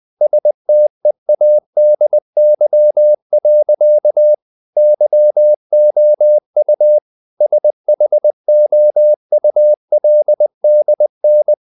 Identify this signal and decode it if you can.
Morse code, heard as a keyed tone, about 20 wpm, 600 Hz. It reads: STEADY. YOU SHOULDN
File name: the old man and the sea / the old man and the sea part 1 20wpm